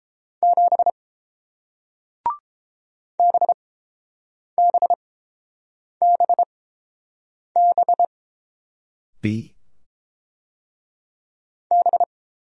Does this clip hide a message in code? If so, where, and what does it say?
Yes: Morse code from 0.4 to 12.0 s, heard as a keyed tone, about 30 wpm, spaced as for 7 wpm, 700 Hz. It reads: 7 BBBB B